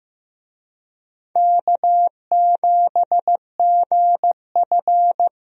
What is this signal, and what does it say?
Morse code, heard as a keyed tone, about 15 wpm, 700 Hz. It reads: K7GF
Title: USA Call Signs - 4 Characters - Repeated 3x - 15wpm